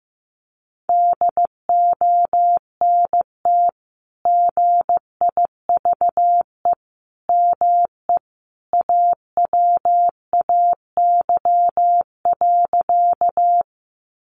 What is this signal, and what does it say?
Morse code, heard as a keyed tone, about 15 wpm, 700 Hz. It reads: DONT GIVE ME AWAY.
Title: Sentences from Top 600 Words 15wpm